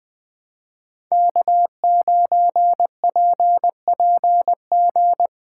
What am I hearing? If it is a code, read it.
Morse code, heard as a keyed tone, about 20 wpm, 700 Hz. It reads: K9PPG
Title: USA Call Signs - 5 Characters - 3x Character Spacing - 20wpm